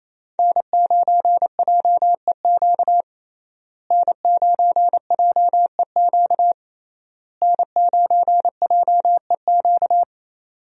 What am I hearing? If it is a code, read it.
Morse code, heard as a keyed tone, about 27 wpm, 700 Hz. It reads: N9JEQ N9JEQ N9JEQ
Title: USA Call Signs - 5 Characters - Repeated 3x - 28wpm